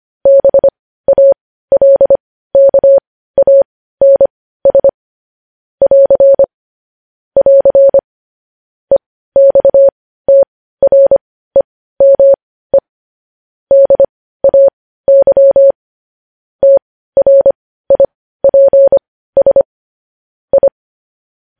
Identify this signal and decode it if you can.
Morse code, heard as a keyed tone, about 25 wpm, spaced as for 15 wpm, 550 Hz. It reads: BALKANS <AR> <AR> EXTREME DAY TRIPS I
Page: news headlines in morse code at 15 wpm